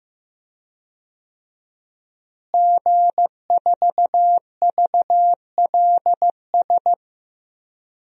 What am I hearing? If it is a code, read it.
Morse code, heard as a keyed tone, about 15 wpm, 700 Hz. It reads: G4VLS
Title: DXCC-100 Call Signs - 5 Characters - Repeated 2x - 15wpm